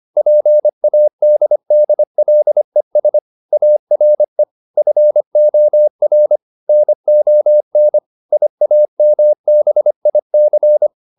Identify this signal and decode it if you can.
Morse code, heard as a keyed tone, about 25 wpm, 600 Hz. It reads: PADDLES ARE FOR NON IAMBIC